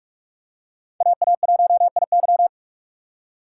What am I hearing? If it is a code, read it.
Morse code, heard as a keyed tone, about 44 wpm, 700 Hz. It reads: AA1IY